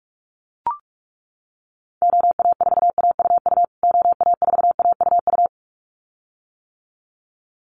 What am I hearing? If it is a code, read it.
Morse code, heard as a keyed tone, about 44 wpm, 700 Hz. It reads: OA4AUU OA4AUU
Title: USA Call Signs Repeated 2x 45wpm